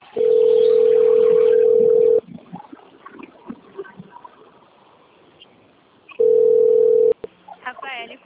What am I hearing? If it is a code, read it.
Ringback tone (North America), 440 + 480 Hz, with speech.
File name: rg-1600-unknown- - - .49583